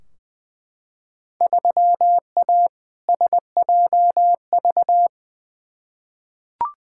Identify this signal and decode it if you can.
Morse code, heard as a keyed tone, about 21 wpm, 700 Hz. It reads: 3A SJV
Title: Field Day Exchanges Encore - 20wpm